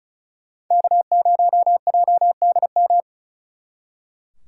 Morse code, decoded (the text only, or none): K0JDM